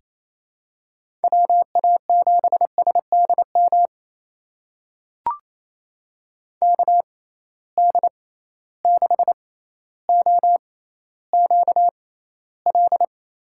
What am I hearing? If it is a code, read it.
Morse code, heard as a keyed tone, about 27 wpm, spaced as for 14 wpm, 700 Hz. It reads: WA7SDM KD6OQL
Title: USA Call Signs - 6 Characters - 6x Character Spacing - 28wpm